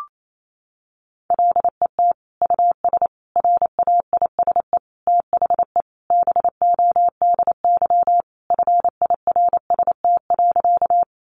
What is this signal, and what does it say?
Morse code, heard as a keyed tone, about 28 wpm, 700 Hz. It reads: LET US RAISE THE BODY FIRST.
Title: Sentences from Top 1400 Words 28wpm